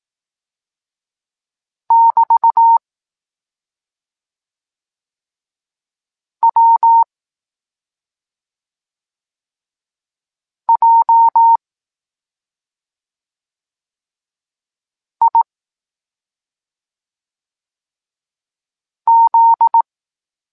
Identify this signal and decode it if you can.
Morse code, heard as a keyed tone, about 18 wpm, 920 Hz. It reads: <BT> W J I Z
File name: Tekst-26 1